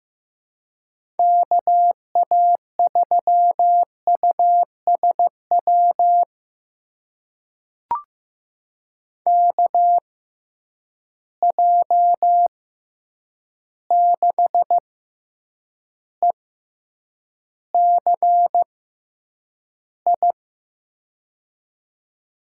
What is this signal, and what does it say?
Morse code, heard as a keyed tone, about 15 wpm, spaced as for 8 wpm, 700 Hz. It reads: KA3USW KJ6ECI
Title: USA Call Signs - 6 Characters - 6x Character Spacing - 15wpm